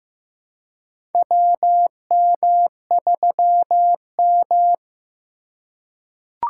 Morse code, heard as a keyed tone, about 15 wpm, 700 Hz. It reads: WM3M
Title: USA Call Signs - 4 Characters - 4x Character Spacing - 15wpm